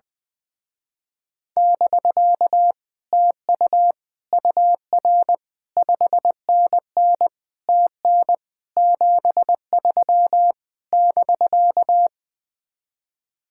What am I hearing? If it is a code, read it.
Morse code, heard as a keyed tone, about 21 wpm, 700 Hz. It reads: <BK> TU UR 5NN TN 73 <BK>
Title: Speed-Racing Common QSO Phrases - 20wpm